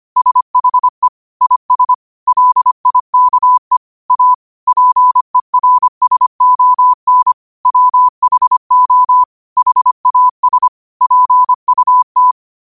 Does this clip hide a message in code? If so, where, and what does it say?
Yes: Morse code from 0.2 to 12.3 s, heard as a keyed tone, about 25 wpm, 1000 Hz. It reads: SHE IS LIKE A PERSON WHO HAS PUT